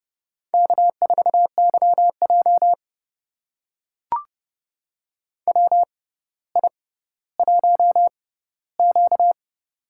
Morse code, heard as a keyed tone, about 30 wpm, spaced as for 15 wpm, 700 Hz. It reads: K4YJ WI1Q
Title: USA Call Signs - 4 Characters - 6x Character Spacing - 30wpm